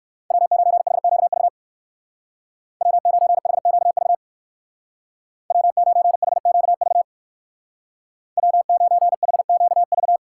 Morse code, heard as a keyed tone, about 56 wpm, 700 Hz. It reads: W9HQV W9HQV W9HQV W9HQV
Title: Speed Racing - USA Call Signs 45wpm